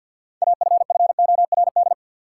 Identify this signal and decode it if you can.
Morse code, heard as a keyed tone, about 53 wpm, 700 Hz. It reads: AFFORD
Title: Six Letter Words - 50wpm